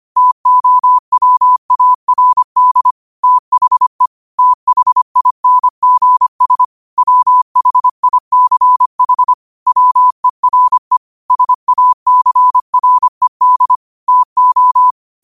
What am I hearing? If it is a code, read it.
Morse code, heard as a keyed tone, about 25 wpm, 1000 Hz. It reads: TOWARD THE THINGS WHICH WERE SACRED TO
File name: qotdfiles / 25wpm